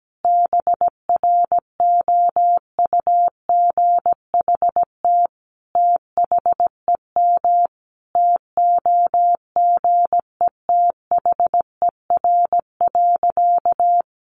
Morse code, heard as a keyed tone, about 17 wpm, 700 Hz. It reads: BROUGHT THEM TOGETHER.
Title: Sentences from Top 1000 Words 17wpm